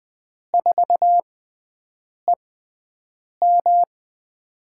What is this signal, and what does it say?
Morse code, heard as a keyed tone, about 20 wpm, spaced as for 7 wpm, 700 Hz. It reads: 4EM
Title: USA Call Signs - 5 Characters - 6x Character Spacing - 20wpm